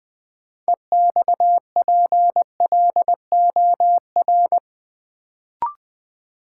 Morse code, heard as a keyed tone, about 20 wpm, 700 Hz. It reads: EXPLOR
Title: Sentences from Top 1100 Words - New Word Drill - 20wpm